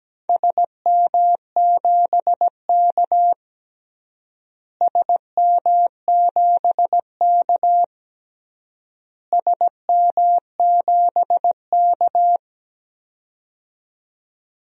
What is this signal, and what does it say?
Morse code, heard as a keyed tone, about 17 wpm, 700 Hz. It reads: SM7K SM7K SM7K
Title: DXCC-100 Call Signs - 4 Characters - Repeated 3x - 17wpm